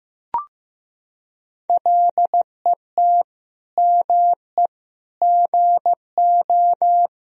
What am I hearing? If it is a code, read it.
Morse code, heard as a keyed tone, about 15 wpm, 700 Hz. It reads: LET ME GO
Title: Sentences from Top 1000 Words - Review Part 4 - 15wpm